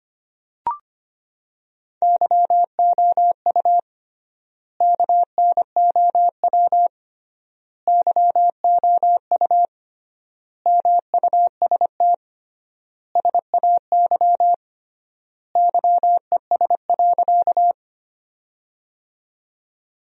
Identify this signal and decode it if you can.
Morse code, heard as a keyed tone, about 25 wpm, 700 Hz. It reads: YOU KNOW YOU MUST SAY YES.
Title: 3x Word-Spacing - Sentences from Top 500 Words - 25wpm